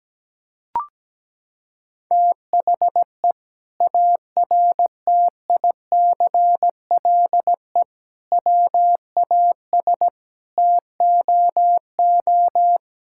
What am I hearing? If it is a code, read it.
Morse code, heard as a keyed tone, about 17 wpm, 700 Hz. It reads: THE ARTICLE WAS TOO
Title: Sentences from Top 1000 Words - Review Part 5 - 17wpm